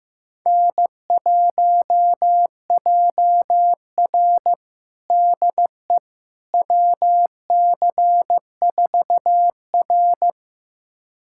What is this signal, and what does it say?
Morse code, heard as a keyed tone, about 16 wpm, 700 Hz. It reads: N1JR DE WC4R
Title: USA Callsign DE Callsign - 4 Characters - 15wpm